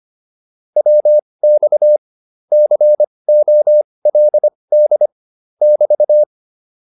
Morse code, heard as a keyed tone, about 25 wpm, spaced as for 20 wpm, 600 Hz. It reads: WX COLD <BT>